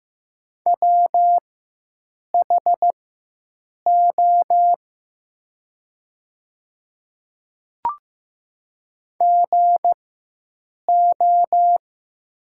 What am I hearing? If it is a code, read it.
Morse code, heard as a keyed tone, about 15 wpm, spaced as for 7 wpm, 700 Hz. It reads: WHO GO
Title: 4x Character Spacing - Top 100 Words - 15wpm